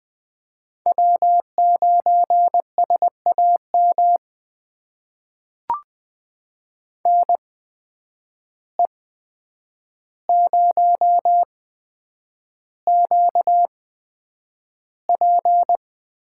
Morse code, heard as a keyed tone, about 20 wpm, spaced as for 9 wpm, 700 Hz. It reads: W9SAM NE0QP
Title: USA Call Signs - 5 Characters - 8x Character Spacing - 20wpm